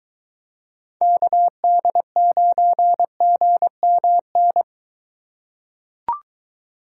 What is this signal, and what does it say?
Morse code, heard as a keyed tone, about 23 wpm, 700 Hz. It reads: KD9GMN